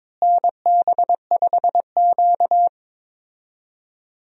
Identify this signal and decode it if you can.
Morse code, heard as a keyed tone, about 22 wpm, 700 Hz. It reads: NB5Q
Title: USA Call Signs - 4 Characters - Repeated 3x - 22wpm